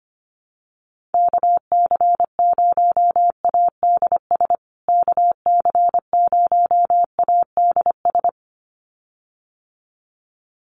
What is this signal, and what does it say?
Morse code, heard as a keyed tone, about 25 wpm, 700 Hz. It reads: KC0ADS KC0ADS
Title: USA Call Signs Repeated 2x 25wpm